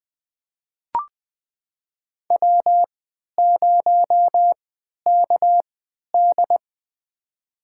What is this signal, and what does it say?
Morse code, heard as a keyed tone, about 20 wpm, spaced as for 11 wpm, 700 Hz. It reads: W0KD
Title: USA Call Signs - 4 Characters - 3x Character Spacing - 20wpm